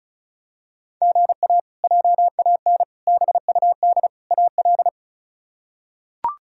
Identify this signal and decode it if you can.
Morse code, heard as a keyed tone, about 36 wpm, 700 Hz. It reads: GA JAN BUD AL
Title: K1USN SST QSO Practice - 35wpm